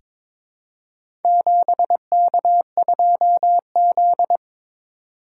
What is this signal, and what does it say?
Morse code, heard as a keyed tone, about 22 wpm, 700 Hz. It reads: 7K2Z
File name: practice / 22wpm / Four Letter-Number Combinations - 22wpm